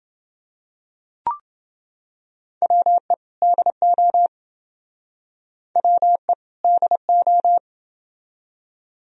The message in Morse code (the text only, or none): WE DO WE DO